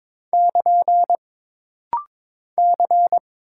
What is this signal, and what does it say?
Morse code, heard as a keyed tone, about 22 wpm, 700 Hz. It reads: <KN> C
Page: This rapid-fire timing is geared to push your ICR (Instant Character Recognition) skills.